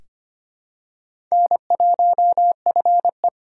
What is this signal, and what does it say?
Morse code, heard as a keyed tone, about 25 wpm, 700 Hz. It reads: N1FE
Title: K1USN SST QSO Practice - 25wpm